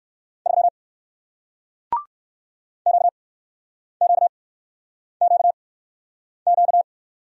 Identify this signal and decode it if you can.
Morse code, heard as a keyed tone, about 59 wpm, spaced as for 9 wpm, 700 Hz. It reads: 3 QQQQ